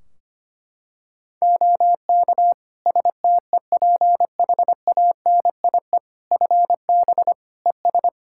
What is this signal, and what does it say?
Morse code, heard as a keyed tone, about 25 wpm, 700 Hz. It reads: OK STEPHANIE FB ES
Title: Practice QSO - Exchange 2 - Info - 25wpm